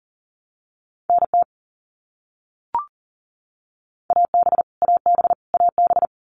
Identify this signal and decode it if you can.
Morse code, heard as a keyed tone, about 40 wpm, 700 Hz. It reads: NT AB AB AB
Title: Canadian Provinces and Territories - Repeated 3x - 40wpm